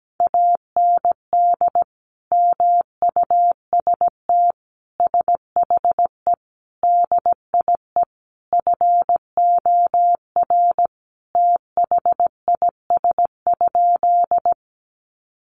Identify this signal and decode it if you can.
Morse code, heard as a keyed tone, about 17 wpm, 700 Hz. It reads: AND MUST SHE DIE FOR THIS?
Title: Sentences from Top 1300 Words 17wpm